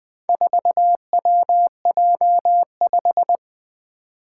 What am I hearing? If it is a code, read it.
Morse code, heard as a keyed tone, about 20 wpm, 700 Hz. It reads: 4WJ5
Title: Four Challenging Letter-Number Combinations 20wpm